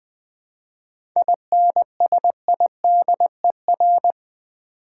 Morse code, heard as a keyed tone, about 20 wpm, 700 Hz. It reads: INSIDER
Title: 7 Letter Words - First 3 Letters Given - 20wpm